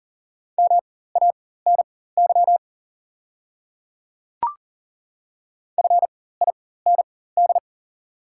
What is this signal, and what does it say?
Morse code, heard as a keyed tone, about 40 wpm, spaced as for 19 wpm, 700 Hz. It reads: MANY FIND
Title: 4x Character Spacing - Top 100 Words - 40wpm